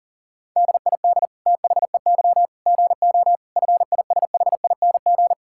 Morse code, heard as a keyed tone, about 40 wpm, 700 Hz. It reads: DID THEY GO FISHING
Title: Sentences from Top 1000 Words - Review Part 5 - 40wpm